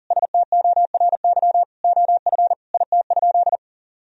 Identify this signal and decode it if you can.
Morse code, heard as a keyed tone, about 40 wpm, 700 Hz. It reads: STORY OF IT?
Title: Sentences from Top 1000 Words - Review Part 1 - 40wpm